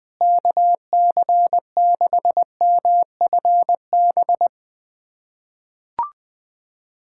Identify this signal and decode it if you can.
Morse code, heard as a keyed tone, about 21 wpm, 700 Hz. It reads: KC6MFB
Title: USA Callsign DE Callsign - 6 Characters - 20wpm